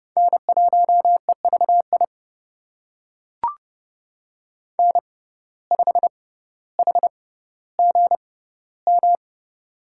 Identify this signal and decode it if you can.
Morse code, heard as a keyed tone, about 30 wpm, spaced as for 15 wpm, 700 Hz. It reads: N1EVI N5HGM
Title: USA Call Signs - 6x Character Spacing - 30wpm